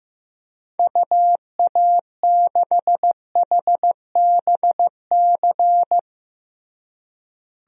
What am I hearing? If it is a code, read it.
Morse code, heard as a keyed tone, about 15 wpm, 700 Hz. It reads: UA6HBC
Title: DXCC-100 Call Signs - 6 Characters - Repeated 2x - 15wpm